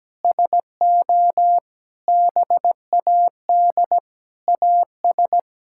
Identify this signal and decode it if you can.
Morse code, heard as a keyed tone, about 17 wpm, 700 Hz. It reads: SO BAD AS
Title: Sentences from Top 1000 Words - Review Part 6 - 17wpm